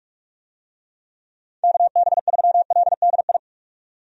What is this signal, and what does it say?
Morse code, heard as a keyed tone, about 46 wpm, 700 Hz. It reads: KB3LDI
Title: USA Call Signs - 6 Characters - 45wpm